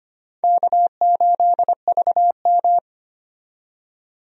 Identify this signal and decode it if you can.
Morse code, heard as a keyed tone, about 25 wpm, 700 Hz. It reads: K8VM
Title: USA Call Signs - 4 Characters - 25wpm